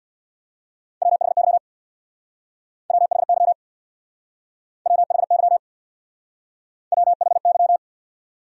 Morse code, heard as a keyed tone, about 61 wpm, 700 Hz. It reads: WHY WHY WHY WHY
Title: Speed-Racing Top 300 Words 50wpm